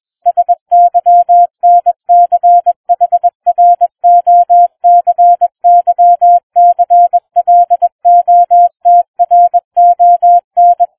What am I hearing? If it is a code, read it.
Morse code, heard as a keyed tone, about 21 wpm, 700 Hz. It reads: SYNCHROCYCLOTRON